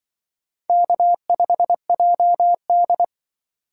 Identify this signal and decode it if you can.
Morse code, heard as a keyed tone, about 24 wpm, 700 Hz. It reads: K5JD
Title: Speed Racing - USA 1x2 Call Signs 20wpm